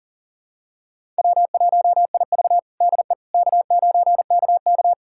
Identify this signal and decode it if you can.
Morse code, heard as a keyed tone, about 40 wpm, 700 Hz. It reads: W1IV DE K9KK
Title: USA Callsign DE Callsign - 4 Characters - 40wpm